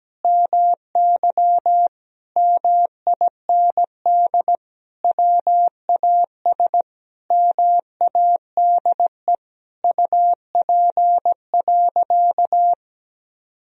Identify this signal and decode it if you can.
Morse code, heard as a keyed tone, about 17 wpm, 700 Hz. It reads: MY MIND WAS MADE UP.
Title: Sentences from Top 1000 Words - Review Part 2 - 17wpm